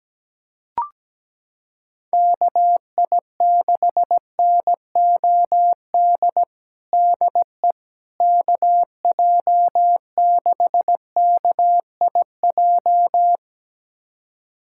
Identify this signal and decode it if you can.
Morse code, heard as a keyed tone, about 17 wpm, 700 Hz. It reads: KI6NOD DE KJ6KIJ
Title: USA Callsign DE Callsign - 6 Characters - 17wpm